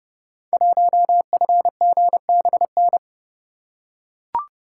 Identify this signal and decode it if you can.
Morse code, heard as a keyed tone, about 30 wpm, 700 Hz. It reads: 1FGBN
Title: Code Groups - 5 Letter:Numbers - 30wpm